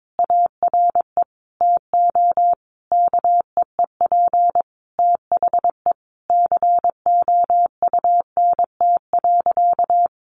Morse code, heard as a keyed tone, about 22 wpm, 700 Hz. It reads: ARE TO KEEP THE COUNT.
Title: Sentences from Top 1500 Words 22wpm